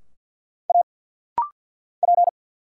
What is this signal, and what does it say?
Morse code, heard as a keyed tone, about 50 wpm, 700 Hz. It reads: A P
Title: Single Letter-Number Alt - Rapid-Fire 50wpm